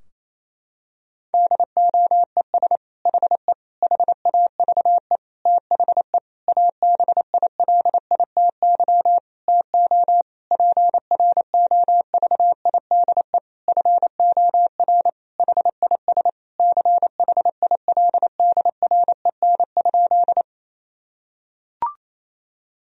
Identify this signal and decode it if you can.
Morse code, heard as a keyed tone, about 28 wpm, 700 Hz. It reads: DOES HE HAVE THE ABILITY TO PROVIDE FOR HIS CHILDREN?
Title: Sentences from Top 1000 Words - Review Part 3 - 28wpm